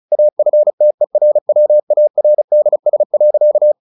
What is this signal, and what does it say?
Morse code, heard as a keyed tone, about 34 wpm, 600 Hz. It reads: AFTERWARDS.
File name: the old man and the sea 2 35wpm